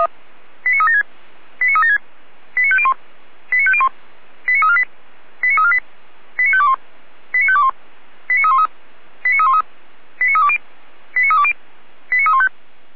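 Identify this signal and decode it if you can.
Five-tone selective call (ZVEI), heard as quick runs of five tones, tones of about 70 ms. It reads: 89376 89376 89501 89501 89348 89348 89521 89521 89313 89313 89320 89320 89315
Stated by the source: ZVEI five-tone call 89305 received by SDR, fire brigade Feuerwehr Orlamünde.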